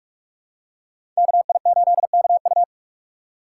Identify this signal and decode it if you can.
Morse code, heard as a keyed tone, about 46 wpm, 700 Hz. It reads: KI8KU
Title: USA Call Signs - 2x Character Spacing - 45wpm